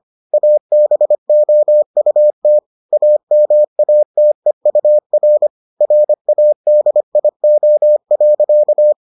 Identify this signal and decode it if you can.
Morse code, heard as a keyed tone, about 25 wpm, 600 Hz. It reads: ABOUT AMATEUR RADIO.